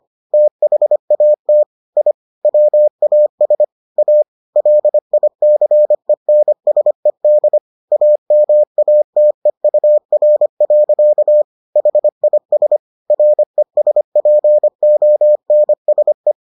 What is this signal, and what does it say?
Morse code, heard as a keyed tone, about 25 wpm, 600 Hz. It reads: THAT I WAS A LICENSED AMATEUR. HIS RESPONSE